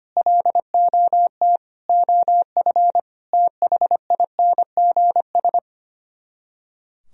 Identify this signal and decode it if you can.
Morse code, heard as a keyed tone, about 33 wpm, 700 Hz. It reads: LOT OF THINGS